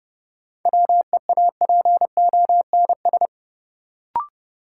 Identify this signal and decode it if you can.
Morse code, heard as a keyed tone, about 32 wpm, 700 Hz. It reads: WEAPONS
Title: Sets of 5 Words Encore - New Word Drill - 30wpm